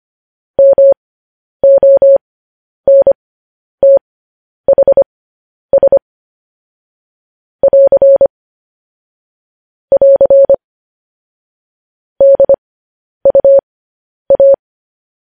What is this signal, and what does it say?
Morse code, heard as a keyed tone, about 25 wpm, spaced as for 10 wpm, 550 Hz. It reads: MONTHS <AR> <AR> DUA
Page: news-headlines-in-morse-code-at-10-wpm